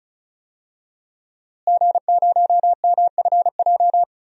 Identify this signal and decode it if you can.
Morse code, heard as a keyed tone, about 36 wpm, 700 Hz. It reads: G0MFJ
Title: DXCC-100 Call Signs - 5 Characters - Repeated 2x - 35wpm